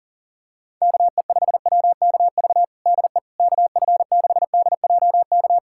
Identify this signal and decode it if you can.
Morse code, heard as a keyed tone, about 40 wpm, 700 Hz. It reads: KE5WKV DE KF6DJK
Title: USA Callsign DE Callsign - 6 Characters - 40wpm